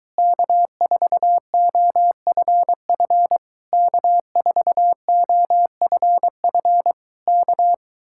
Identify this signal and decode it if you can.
Morse code, heard as a keyed tone, about 23 wpm, 700 Hz. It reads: K4OFF K4OFF K